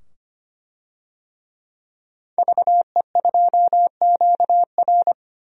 Morse code, heard as a keyed tone, about 25 wpm, 700 Hz. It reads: VE2QR
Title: DXCC-100 Call Signs - 5 Characters - 25wpm